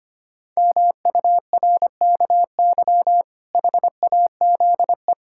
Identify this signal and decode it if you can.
Morse code, heard as a keyed tone, about 25 wpm, 700 Hz. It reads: MURKY HAZE
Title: Pangram Challenge v2 - 25wpm